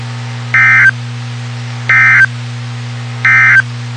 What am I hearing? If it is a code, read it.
Emergency Alert System (SAME) tones, heard as harsh buzzing bursts. It is heard from 0.4 to 3.7 s.
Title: WXK91 NOAA Weather Radio